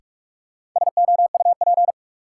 Morse code, heard as a keyed tone, about 42 wpm, 700 Hz. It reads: SOUP
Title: Sentences from Top 1600 Words - New Word Drill Alt - 45wpm